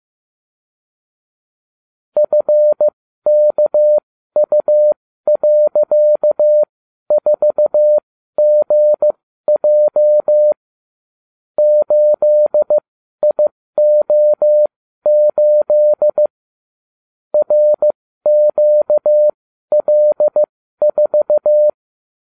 Morse code, heard as a keyed tone, about 15 wpm, spaced as for 12 wpm, 600 Hz. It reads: FKU.4GJ 8IO8 RQL4
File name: cw / ubacw34 3